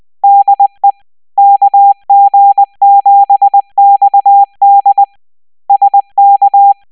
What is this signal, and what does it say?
Morse code, heard as a keyed tone, about 20 wpm, 800 Hz. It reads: DE KG7XD SK